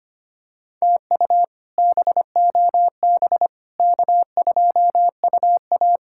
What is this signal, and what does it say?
Morse code, heard as a keyed tone, about 25 wpm, 700 Hz. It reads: TU BOB K2UA
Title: K1USN SST QSO Practice - 25wpm